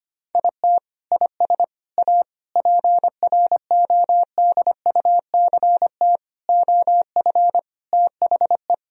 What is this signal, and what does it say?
Morse code, heard as a keyed tone, about 25 wpm, 700 Hz. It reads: IT IS A PRODUCT OF THE
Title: Sentences from Top 1000 Words - Review Part 4 - 25wpm